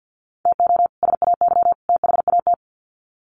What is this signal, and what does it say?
Morse code, heard as a keyed tone, about 50 wpm, 700 Hz. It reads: TO SAY THAT
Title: Sets of 3 Words 50wpm